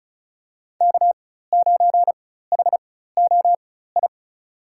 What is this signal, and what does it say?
Morse code, heard as a keyed tone, about 34 wpm, spaced as for 16 wpm, 700 Hz. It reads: K9HOI
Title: USA Call Signs - 4x Character Spacing - 35wpm